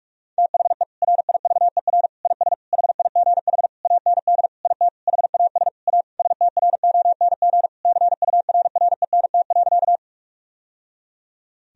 Morse code, heard as a keyed tone, about 46 wpm, 700 Hz. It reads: THE RIVER IS HIGH AND IT HAS A STRONG CURRENT.
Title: Sentences from Top 1000 Words - Review Part 4 - 45wpm